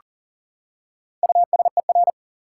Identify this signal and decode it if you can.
Morse code, heard as a keyed tone, about 40 wpm, 700 Hz. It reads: USER
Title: Four Letter Words - 40wpm